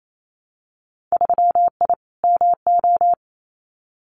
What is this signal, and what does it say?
Morse code, heard as a keyed tone, about 28 wpm, 700 Hz. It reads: 3I MO
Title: Winter Field Day Exchanges 28wpm